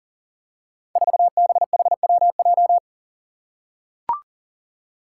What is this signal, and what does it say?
Morse code, heard as a keyed tone, about 40 wpm, 700 Hz. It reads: 4BHWJ
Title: Five Challenging Letter-Number Combinations 40wpm